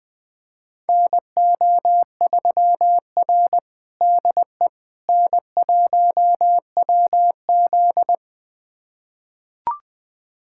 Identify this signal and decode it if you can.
Morse code, heard as a keyed tone, about 21 wpm, 700 Hz. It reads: NO3R DE N1WZ
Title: USA Callsign DE Callsign - 4 Characters - 20wpm